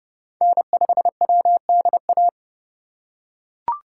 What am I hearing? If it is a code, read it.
Morse code, heard as a keyed tone, about 30 wpm, 700 Hz. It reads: N5WDA
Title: USA Call Signs - 5 Characters - 2x Character Spacing - 30wpm